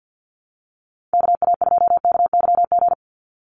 Morse code, heard as a keyed tone, about 50 wpm, 700 Hz. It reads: KA2KYG
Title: USA Call Signs 50wpm